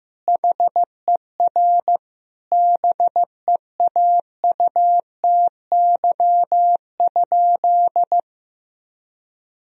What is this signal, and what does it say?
Morse code, heard as a keyed tone, about 16 wpm, 700 Hz. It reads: HER BEAUTY?
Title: Sentences from Top 1000 Words - Review Part 2 - 15wpm